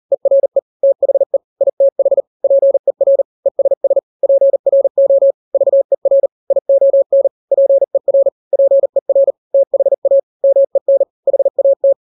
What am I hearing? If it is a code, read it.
Morse code, heard as a keyed tone, about 40 wpm, 550 Hz. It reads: ERE THE ITH PER ESS PRO VER ION PER PER THA MEN HAT
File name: n-grams / 3-grams 40 35 WPM